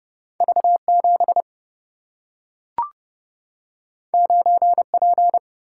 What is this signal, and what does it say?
Morse code, heard as a keyed tone, about 30 wpm, 700 Hz. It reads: V7 9P
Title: Single-Letter Single-Number 30wpm